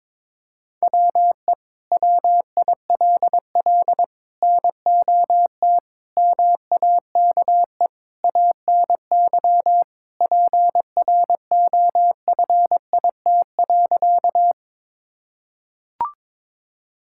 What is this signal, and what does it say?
Morse code, heard as a keyed tone, about 22 wpm, 700 Hz. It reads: WE WILL NOT MAKE ANY PROFIT.
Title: Sentences from Top 1000 Words - Review Part 5 - 22wpm